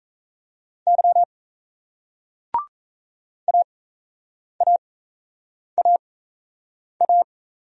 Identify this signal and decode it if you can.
Morse code, heard as a keyed tone, about 37 wpm, spaced as for 8 wpm, 700 Hz. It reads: Y AAAA